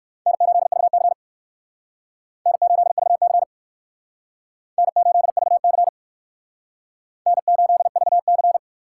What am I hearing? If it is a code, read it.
Morse code, heard as a keyed tone, about 59 wpm, 700 Hz. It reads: N8VC N8VC N8VC N8VC